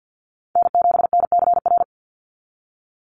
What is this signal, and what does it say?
Morse code, heard as a keyed tone, about 50 wpm, 700 Hz. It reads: N7NCR
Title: USA Call Signs 50wpm